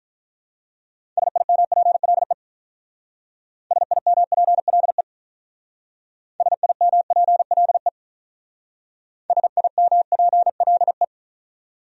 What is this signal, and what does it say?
Morse code, heard as a keyed tone, about 41 wpm, 700 Hz. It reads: SIMPLE SIMPLE SIMPLE SIMPLE
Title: Speed-Racing Top 500 Words 35wpm